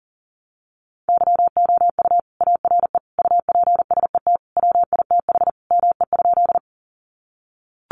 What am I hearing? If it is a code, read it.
Morse code, heard as a keyed tone, about 40 wpm, 700 Hz. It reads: YOU ARE UPSET WITH ME?
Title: Sentences from Top 800 Words 40wpm